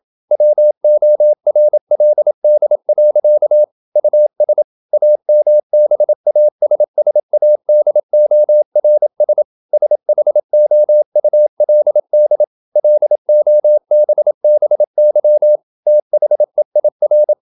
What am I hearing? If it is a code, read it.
Morse code, heard as a keyed tone, about 27 wpm, 600 Hz. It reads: WORLD. US AMBASSADORS SHOULD LOBBY THEIR
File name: Never 0027wpm